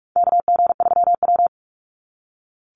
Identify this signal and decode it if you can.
Morse code, heard as a keyed tone, about 44 wpm, 700 Hz. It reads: KG3W
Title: NA SOTA Chaser Callsigns 45wpm